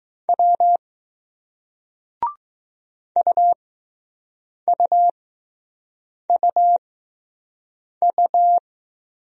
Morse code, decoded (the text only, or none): W UUUU